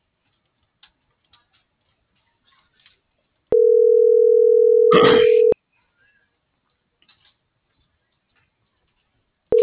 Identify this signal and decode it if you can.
Ringback tone (North America), 440 + 480 Hz.
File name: CENTRAL NAVIERAS / out oscar.monta.o